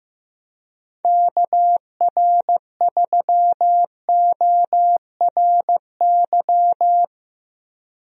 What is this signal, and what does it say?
Morse code, heard as a keyed tone, about 15 wpm, 700 Hz. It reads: KR3ORY
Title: USA Call Signs - 6 Characters - 15wpm